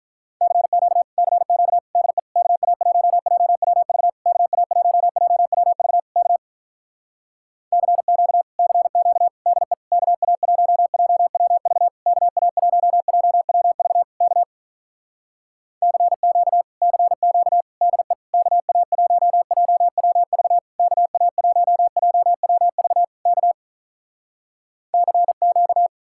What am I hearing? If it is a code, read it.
Morse code, heard as a keyed tone, about 46 wpm, 700 Hz. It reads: CQ CQ DE KA1JWV KA1JWV K CQ CQ DE KA1JWV KA1JWV K CQ CQ DE KA1JWV KA1JWV K CQ